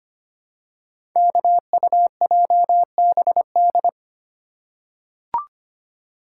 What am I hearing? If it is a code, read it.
Morse code, heard as a keyed tone, about 25 wpm, 700 Hz. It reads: KUJBD
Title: Code Groups - 5 Letter:Numbers - 25wpm